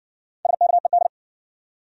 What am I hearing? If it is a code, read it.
Morse code, heard as a keyed tone, about 53 wpm, 700 Hz. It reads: SKED